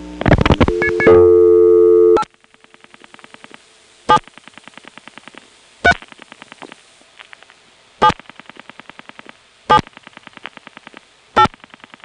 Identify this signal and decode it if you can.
DTMF, heard as telephone touch tones, tones of about 70 ms. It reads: 8 7 3 7 7 8
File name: phonetrips / SmyrnaGA5HQ